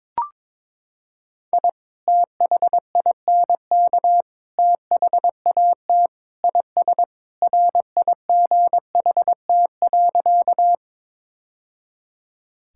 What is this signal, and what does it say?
Morse code, heard as a keyed tone, about 22 wpm, 700 Hz. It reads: I THINK THAT IS RIGHT.
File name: practice / 22wpm / Sentences with 3-Word Phrases - 22wpm